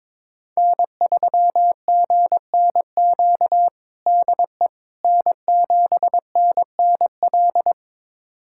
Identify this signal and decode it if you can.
Morse code, heard as a keyed tone, about 22 wpm, 700 Hz. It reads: N3GNQ DE N7NNL
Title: USA Callsign DE Callsign - 5 Characters - 22wpm